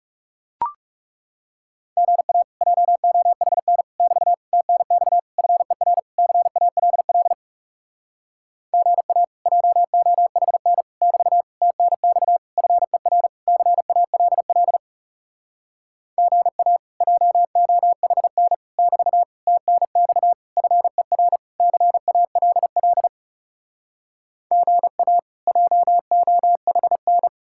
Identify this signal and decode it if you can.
Morse code, heard as a keyed tone, about 40 wpm, 700 Hz. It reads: GA JOHN <BT> TNX FER CALL GA JOHN <BT> TNX FER CALL GA JOHN <BT> TNX FER CALL GA JOHN